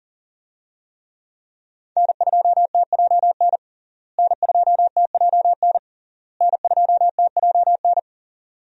Morse code, heard as a keyed tone, about 40 wpm, 700 Hz. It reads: N2TJN N2TJN N2TJN
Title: USA Call Signs - 5 Characters - Repeated 3x - 40wpm